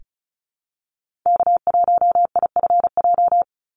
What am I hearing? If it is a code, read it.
Morse code, heard as a keyed tone, about 34 wpm, 700 Hz. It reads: K1IFJ
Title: Call Signs Repeated 3x - 35wpm